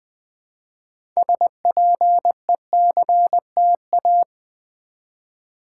Morse code, heard as a keyed tone, about 20 wpm, 700 Hz. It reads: SPECTA
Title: Build A Word - 3 Letters At A Time - 9-Letter Words - 20wpm